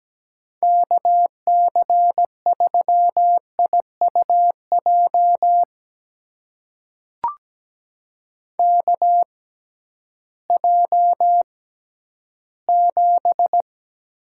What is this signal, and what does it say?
Morse code, heard as a keyed tone, about 17 wpm, spaced as for 11 wpm, 700 Hz. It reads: KC3IUJ KJ7
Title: USA Call Signs - 6 Characters - 6x Character Spacing - 17wpm